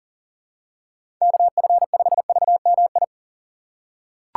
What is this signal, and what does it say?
Morse code, heard as a keyed tone, about 37 wpm, 700 Hz. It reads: KF5VMI
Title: USA Call Signs - 6 Characters - 2x Character Spacing - 40wpm